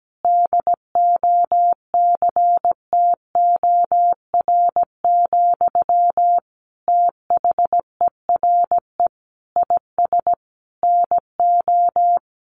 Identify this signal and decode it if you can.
Morse code, heard as a keyed tone, about 17 wpm, 700 Hz. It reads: DOCTOR, THERE IS NO
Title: Sentences from Top 1500 Words 17wpm